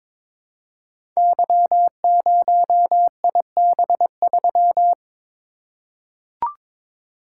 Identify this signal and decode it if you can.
Morse code, heard as a keyed tone, about 22 wpm, 700 Hz. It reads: Y0IB3
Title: Code Groups - 5 Letter:Numbers - 22wpm